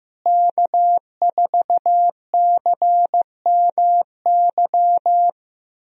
Morse code, heard as a keyed tone, about 15 wpm, 700 Hz. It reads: K4CMY
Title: USA Call Signs - 5 Characters - Repeated 2x - 15wpm